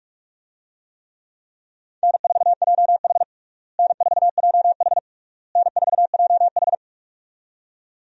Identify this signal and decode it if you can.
Morse code, heard as a keyed tone, about 41 wpm, 700 Hz. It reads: N4JH N4JH N4JH
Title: USA Call Signs - 4 Characters - Repeated 3x - 45wpm